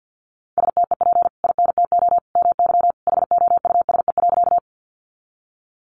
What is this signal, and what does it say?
Morse code, heard as a keyed tone, about 48 wpm, 700 Hz. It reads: STEP INTO MY HOUSE.
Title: Sentences from Top 600 Words 50wpm